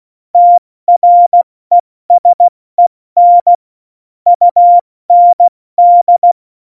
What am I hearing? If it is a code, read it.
Morse code, heard as a keyed tone, about 16 wpm, spaced as for 14 wpm, 700 Hz. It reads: TRESEN UND
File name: Morsekurs / Witze / Witz 22. Dezember 2020, 70 F80